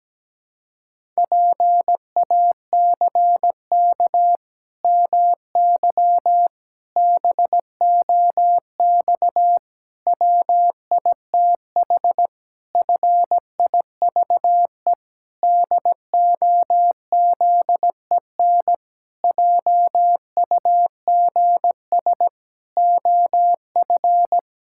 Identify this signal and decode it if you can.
Morse code, heard as a keyed tone, about 17 wpm, 700 Hz. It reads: PACK MY BOX WITH FIVE DOZEN JUGS OF
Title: Pangram Challenge v2 - 17wpm